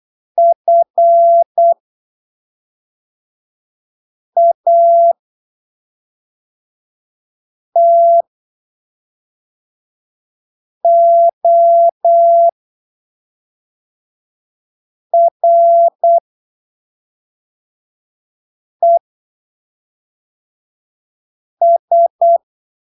Morse code, heard as a keyed tone, about 8 wpm, spaced as for 3 wpm, 680 Hz. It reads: FATORES
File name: arquivos / texto / texto 22 08ppm